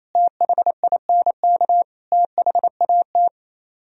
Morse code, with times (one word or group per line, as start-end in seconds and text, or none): 0.1-1.8 THINK
2.1-3.3 THAT